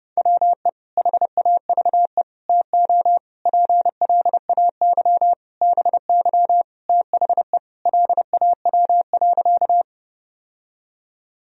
Morse code, heard as a keyed tone, about 30 wpm, 700 Hz. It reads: WE HAVE TO PLAY BY THE LAW.
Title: Sentences from Top 1000 Words - Review Part 3 - 30wpm